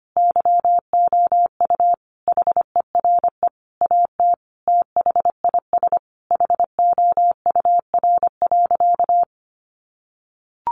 Morse code, heard as a keyed tone, about 25 wpm, 700 Hz. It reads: YOU HERE AT THIS HOUR.
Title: Sentences from Top 1400 Words 25wpm